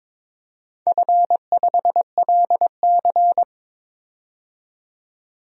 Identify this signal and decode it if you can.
Morse code, heard as a keyed tone, about 21 wpm, 700 Hz. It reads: F5LC
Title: DXCC-100 Call Signs - 4 Characters - 22wpm